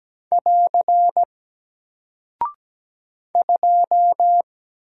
Morse code, heard as a keyed tone, about 17 wpm, 700 Hz. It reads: <AR> 2
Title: Single Character (Letter:Number:Punctuation:Prosign) Alt 17wpm